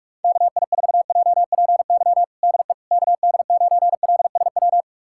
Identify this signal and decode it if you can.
Morse code, heard as a keyed tone, about 46 wpm, 700 Hz. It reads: KI4JPY DE KD9LSW
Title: USA Callsign DE Callsign 45wpm